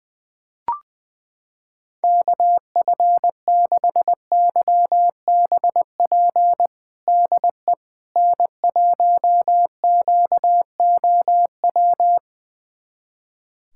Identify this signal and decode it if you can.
Morse code, heard as a keyed tone, about 20 wpm, 700 Hz. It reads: KF6YBP DE N1QOW
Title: USA Callsign DE Callsign 20wpm